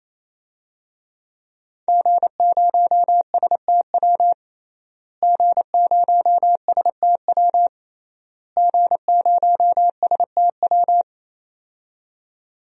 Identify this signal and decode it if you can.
Morse code, heard as a keyed tone, about 28 wpm, 700 Hz. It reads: G0STW G0STW G0STW
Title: DXCC-100 Call Signs - 5 Characters - Repeated 3x - 28wpm